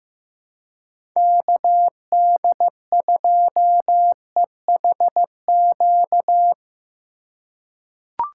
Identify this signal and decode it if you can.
Morse code, heard as a keyed tone, about 15 wpm, 700 Hz. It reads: KD2EHQ
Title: USA Call Signs - 6 Characters - 4x Character Spacing - 15wpm